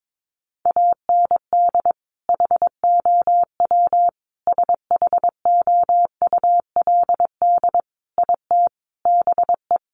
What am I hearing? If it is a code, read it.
Morse code, heard as a keyed tone, about 22 wpm, 700 Hz. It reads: AND HOW SHOULD IT BE
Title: Sentences from Top 1200 Words 22wpm